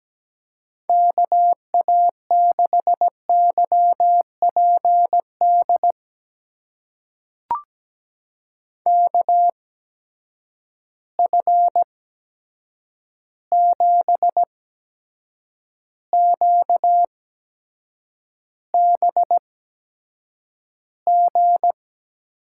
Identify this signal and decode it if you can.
Morse code, heard as a keyed tone, about 17 wpm, spaced as for 7 wpm, 700 Hz. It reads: KA6YPD KF7QBG
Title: USA Call Signs - 6 Characters - 8x Character Spacing - 17wpm